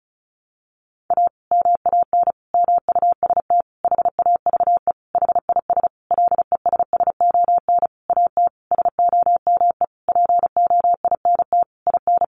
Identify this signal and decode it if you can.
Morse code, heard as a keyed tone, about 34 wpm, 700 Hz. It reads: A MAN MUST HAVE HIS LESSON AT SOME POINT IN
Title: Sentences from Top 1200 Words 35wpm